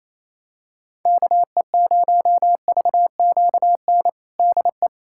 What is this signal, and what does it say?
Morse code, heard as a keyed tone, about 28 wpm, 700 Hz. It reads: KE0VQN DE
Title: USA Callsign DE Callsign 28wpm